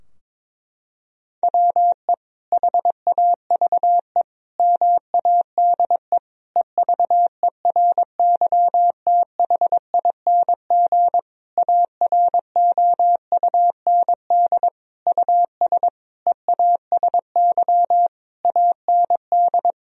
Morse code, heard as a keyed tone, about 22 wpm, 700 Hz. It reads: WE HAVE MADE EVERYTHING AROUND US EASY AND
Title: Sentences from Top 1000 Words - Review Part 3 - 22wpm